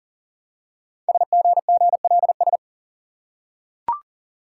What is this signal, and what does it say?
Morse code, heard as a keyed tone, about 39 wpm, 690 Hz. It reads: SGGLS
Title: Code Groups - 5 Letters - 40wpm